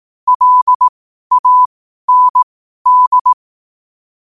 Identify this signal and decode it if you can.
Morse code, heard as a keyed tone, about 18 wpm, spaced as for 12 wpm, 1000 Hz. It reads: LAND
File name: qotdfiles / 12wpm